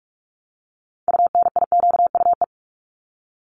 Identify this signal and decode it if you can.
Morse code, heard as a keyed tone, about 44 wpm, 700 Hz. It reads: UNIQUE
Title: Top 751-1000 Words 45wpm